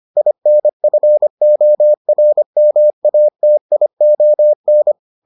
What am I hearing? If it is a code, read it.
Morse code, heard as a keyed tone, about 25 wpm, 600 Hz. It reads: INFORMATION